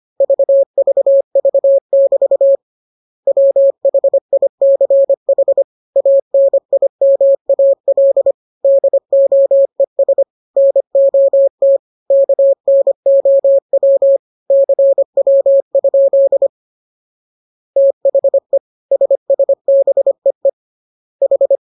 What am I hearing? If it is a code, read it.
Morse code, heard as a keyed tone, about 25 wpm, 550 Hz. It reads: VVV<BT> WHICH ANIMAL DOES NOT KNOW CW? THE SSBEE H